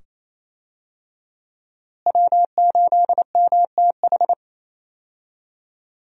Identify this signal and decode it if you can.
Morse code, heard as a keyed tone, about 28 wpm, 700 Hz. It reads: W8MTH
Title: USA Call Signs - 5 Characters - Repeated 2x - 28wpm